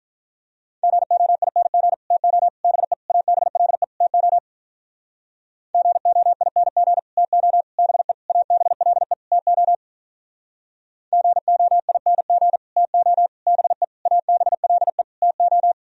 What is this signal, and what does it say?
Morse code, heard as a keyed tone, about 46 wpm, 700 Hz. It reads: GOING TO BE ABLE TO GOING TO BE ABLE TO GOING TO BE ABLE TO